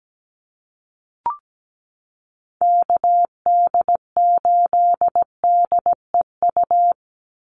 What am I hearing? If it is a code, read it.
Morse code, heard as a keyed tone, about 17 wpm, 700 Hz. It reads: KD8DEU
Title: NA SOTA Chaser Callsigns 17wpm